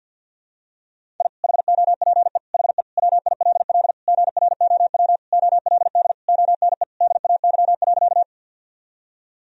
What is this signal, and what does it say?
Morse code, heard as a keyed tone, about 48 wpm, 700 Hz. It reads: I HOPE HE WILL GROW OLD ONE DAY.
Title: Sentences from Top 1000 Words - Review Part 1 - 50wpm